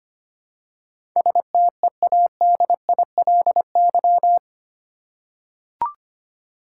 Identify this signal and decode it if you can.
Morse code, heard as a keyed tone, about 25 wpm, 700 Hz. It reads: STEADILY
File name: Eight Letter Words - 25wpm